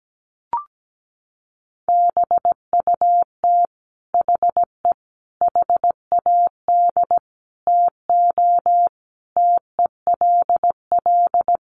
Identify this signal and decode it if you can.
Morse code, heard as a keyed tone, about 17 wpm, 700 Hz. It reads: BUT HE HAD TO TELL
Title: Sentences from Top 800 Words 17wpm